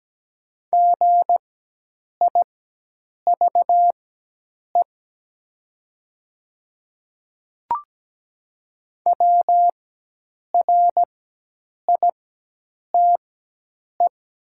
Morse code, heard as a keyed tone, about 17 wpm, spaced as for 8 wpm, 700 Hz. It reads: GIVE WRITE
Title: 4x Character Spacing - Top 200 Words - 17wpm